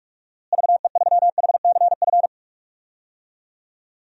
Morse code, heard as a keyed tone, about 46 wpm, 700 Hz. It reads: VE3HCF
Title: DXCC-100 Call Signs - 6 Characters - Repeated 3x - 45wpm